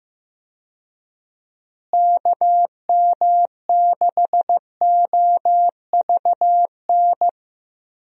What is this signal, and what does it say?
Morse code, heard as a keyed tone, about 15 wpm, 700 Hz. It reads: KM6OVN
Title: USA Call Signs - 6 Characters - Repeated 2x - 15wpm